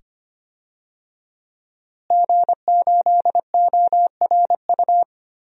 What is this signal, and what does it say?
Morse code, heard as a keyed tone, about 25 wpm, 700 Hz. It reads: G8ORU
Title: DXCC-100 Call Signs - 5 Characters - Repeated 2x - 25wpm